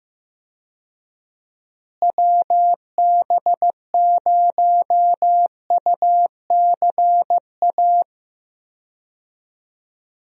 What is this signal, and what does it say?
Morse code, heard as a keyed tone, about 15 wpm, 700 Hz. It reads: WB0UCA
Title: USA Call Signs - 6 Characters - Repeated 2x - 15wpm